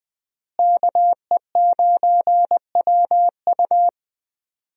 Morse code, heard as a keyed tone, about 20 wpm, 700 Hz. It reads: KE9WU
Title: USA Call Signs - 5 Characters - Repeated 2x - 20wpm